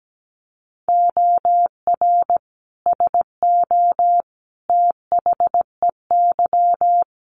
Morse code, heard as a keyed tone, about 17 wpm, 700 Hz. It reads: OR SO THEY
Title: Sentences from Top 100 Words 17wpm